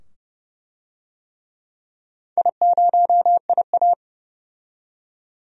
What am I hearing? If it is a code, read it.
Morse code, heard as a keyed tone, about 30 wpm, 700 Hz. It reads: I0IA
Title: DXCC-100 Call Signs - 4 Characters - 30wpm